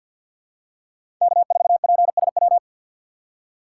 Morse code, heard as a keyed tone, about 50 wpm, 700 Hz. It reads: K4PSW
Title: USA Call Signs - 5 Characters - 6x Character Spacing - 50wpm